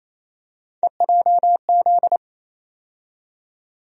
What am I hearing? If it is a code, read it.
Morse code, heard as a keyed tone, about 28 wpm, 700 Hz. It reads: EJZ
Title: USA Call Sign Postfixes - 3 Letters 28wpm